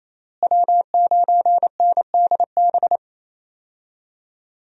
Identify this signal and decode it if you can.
Morse code, heard as a keyed tone, about 28 wpm, 700 Hz. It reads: W9NDB
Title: USA Call Signs - 5 Characters - Repeated 3x - 28wpm